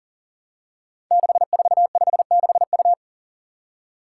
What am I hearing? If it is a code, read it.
Morse code, heard as a keyed tone, about 42 wpm, 700 Hz. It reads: 6456U
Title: Five Challenging Letter-Number Combinations 40wpm